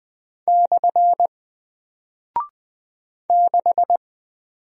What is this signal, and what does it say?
Morse code, heard as a keyed tone, about 20 wpm, 700 Hz. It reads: / 6
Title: Single Character (Letter:Number:Punctuation:Prosign) Alt 20wpm